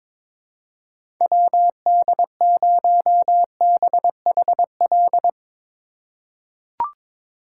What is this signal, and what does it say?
Morse code, heard as a keyed tone, about 22 wpm, 700 Hz. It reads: WD0BHL
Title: USA Call Signs - 3x Character Spacing - 22wpm